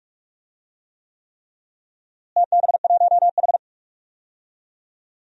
Morse code, heard as a keyed tone, about 46 wpm, 700 Hz. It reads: TB1H
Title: DXCC-100 Call Signs - 4 Characters - Repeated 3x - 45wpm